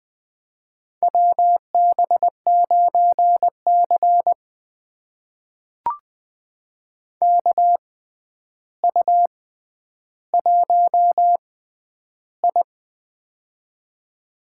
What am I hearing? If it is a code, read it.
Morse code, heard as a keyed tone, about 20 wpm, spaced as for 10 wpm, 700 Hz. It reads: WB9C KU1I
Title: USA Call Signs - 6x Character Spacing - 20wpm